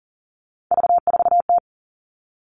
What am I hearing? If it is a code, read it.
Morse code, heard as a keyed tone, about 40 wpm, 700 Hz. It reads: V4T
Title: All Cut Numbers - 3-Digit 40wpm